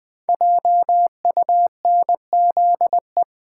Morse code, heard as a keyed tone, about 20 wpm, 700 Hz. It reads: JUNZE
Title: Code Groups - 5 Letters - 20wpm